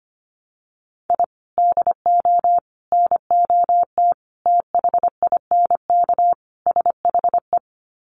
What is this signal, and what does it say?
Morse code, heard as a keyed tone, about 25 wpm, 700 Hz. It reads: I DO NOT THINK SHE
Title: Sets of 5 Words 25wpm